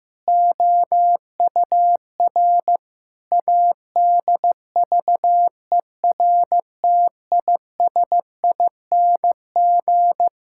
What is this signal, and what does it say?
Morse code, heard as a keyed tone, about 15 wpm, 700 Hz. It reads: OUR ADVERTISING
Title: Sentences from Top 1000 Words - Review Part 6 - 15wpm